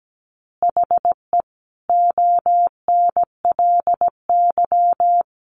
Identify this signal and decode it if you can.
Morse code, heard as a keyed tone, about 17 wpm, 700 Hz. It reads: HE ONLY
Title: Easy Phrases 17wpm